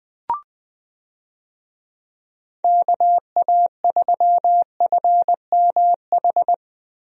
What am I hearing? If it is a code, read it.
Morse code, heard as a keyed tone, about 20 wpm, 700 Hz. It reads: KA3FMH
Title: USA Call Signs - 6 Characters - Repeated 2x - 20wpm